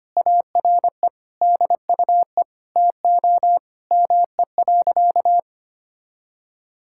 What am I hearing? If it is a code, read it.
Morse code, heard as a keyed tone, about 25 wpm, 700 Hz. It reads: ARE DUE TO ME.
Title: Sentences from Top 1000 Words - Review Part 6 - 25wpm